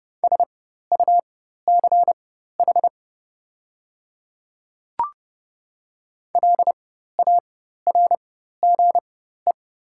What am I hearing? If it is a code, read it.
Morse code, heard as a keyed tone, about 30 wpm, spaced as for 14 wpm, 700 Hz. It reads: SUCH LARGE
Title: 4x Character Spacing - Top 200 Words - 30wpm